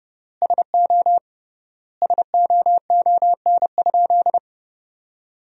Morse code, heard as a keyed tone, about 30 wpm, 700 Hz. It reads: SO SOON?
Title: 3x Word-Spacing - Sentences from Top 400 Words - 30wpm